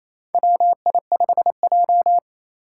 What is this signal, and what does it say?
Morse code, heard as a keyed tone, about 27 wpm, 700 Hz. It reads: WI5J
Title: Four Challenging Letter-Number Combinations 28wpm